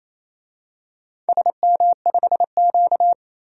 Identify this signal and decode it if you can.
Morse code, heard as a keyed tone, about 28 wpm, 700 Hz. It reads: SM5Q
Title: DXCC-100 Call Signs - 4 Characters - 28wpm